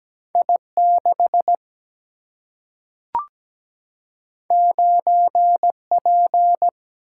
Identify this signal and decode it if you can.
Morse code, heard as a keyed tone, about 17 wpm, 700 Hz. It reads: I6 9P
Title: Single-Letter Single-Number 17wpm